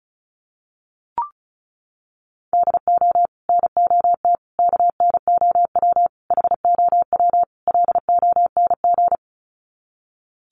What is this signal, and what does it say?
Morse code, heard as a keyed tone, about 34 wpm, 700 Hz. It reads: DO NOT KNOW HOW LONG
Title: Sets of 5 Words Encore2 35wpm